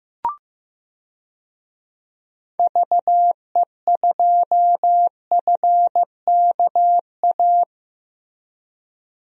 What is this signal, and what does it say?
Morse code, heard as a keyed tone, about 15 wpm, 700 Hz. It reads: VE2FKA
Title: DXCC-100 Call Signs - 6 Characters - Repeated 3x - 15wpm